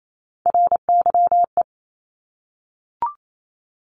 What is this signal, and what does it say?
Morse code, heard as a keyed tone, about 28 wpm, 700 Hz. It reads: RYE
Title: Multi-letter Phonemes 28wpm